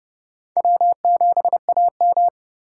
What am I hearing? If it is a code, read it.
Morse code, heard as a keyed tone, about 30 wpm, 700 Hz. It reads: W7AM
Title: USA Call Signs - 4 Characters - 3x Character Spacing - 30wpm